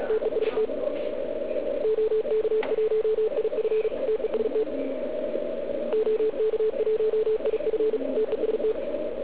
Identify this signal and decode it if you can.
Morse code, heard as a keyed tone, about 36 wpm, 440 Hz. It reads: TEST OK1IF TEST OK1IF TEST